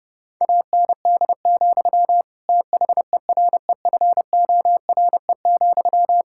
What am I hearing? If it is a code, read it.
Morse code, heard as a keyed tone, about 30 wpm, 700 Hz. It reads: AND, THEREFORE,
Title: Sentences from Top 1000 Words - Review Part 5 - 30wpm